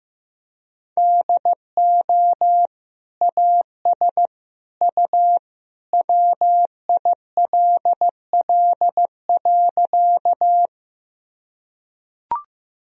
Morse code, heard as a keyed tone, about 15 wpm, 700 Hz. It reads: DO AS U WILL.
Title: Sentences from Top 100 Words and Abbreviations 15wpm